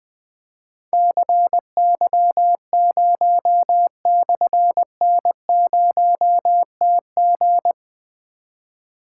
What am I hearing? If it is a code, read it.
Morse code, heard as a keyed tone, about 20 wpm, 700 Hz. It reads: CY0/N0TG
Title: Difficult Call Signs 20wpm